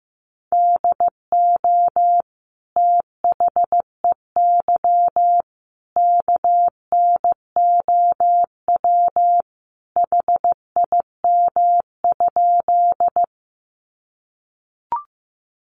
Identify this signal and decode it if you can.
Morse code, heard as a keyed tone, about 14 wpm, 700 Hz. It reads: DO THEY KNOW HIM?
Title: Sentences from Top 100 Words 15wpm